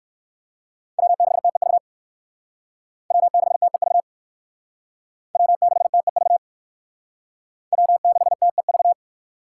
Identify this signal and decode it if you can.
Morse code, heard as a keyed tone, about 57 wpm, 700 Hz. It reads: W6TEV W6TEV W6TEV W6TEV